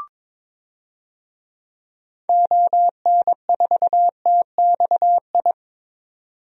Morse code, heard as a keyed tone, about 22 wpm, 700 Hz. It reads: ON4TXI
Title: DXCC-100 Call Signs - 6 Characters - 22wpm